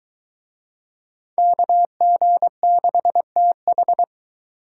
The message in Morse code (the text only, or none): KG6TH